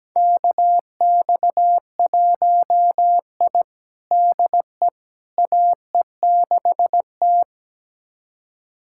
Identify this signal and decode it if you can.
Morse code, heard as a keyed tone, about 17 wpm, 700 Hz. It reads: KX1I DE AE6T
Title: USA Callsign DE Callsign - 4 Characters - 17wpm